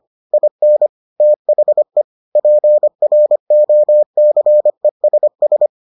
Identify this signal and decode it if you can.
Morse code, heard as a keyed tone, about 25 wpm, 600 Hz. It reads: IN THE PROCESS